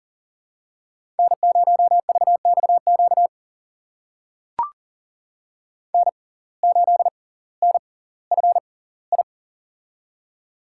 Morse code, heard as a keyed tone, about 38 wpm, spaced as for 20 wpm, 700 Hz. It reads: N0VXQ N8NFI
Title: USA Call Signs - 5 Characters - 6x Character Spacing - 40wpm